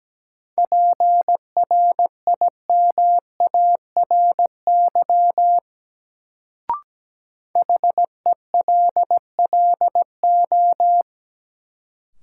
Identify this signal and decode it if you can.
Morse code, heard as a keyed tone, about 17 wpm, 700 Hz. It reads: PRIMARY HELLO
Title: Sentences from Top 1100 Words - New Word Drill Alt - 17wpm